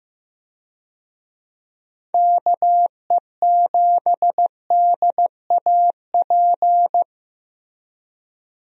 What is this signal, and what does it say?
Morse code, heard as a keyed tone, about 15 wpm, 700 Hz. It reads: KE7DAP
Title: USA Call Signs - 6 Characters - 15wpm